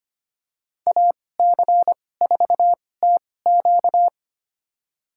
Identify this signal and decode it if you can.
Morse code, heard as a keyed tone, about 25 wpm, spaced as for 18 wpm, 700 Hz. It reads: AC4TQ
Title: USA Call Signs - 2x Character Spacing - 25wpm